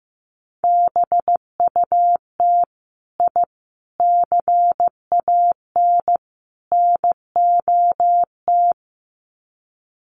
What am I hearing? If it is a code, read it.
Morse code, heard as a keyed tone, about 15 wpm, 700 Hz. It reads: BUT I CAN NOT
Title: Sets of 4 Words 15wpm